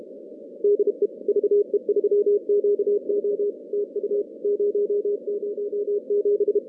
Morse code, heard as a keyed tone, about 32 wpm, 410 Hz. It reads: DE VE3QO TU 007
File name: contests / CW-SS-k5tr- -2100